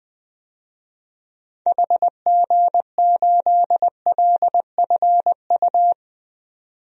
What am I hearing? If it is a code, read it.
Morse code, heard as a keyed tone, about 20 wpm, 700 Hz. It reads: HG8LFU
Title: DXCC-100 Call Signs - 6 Characters - 20wpm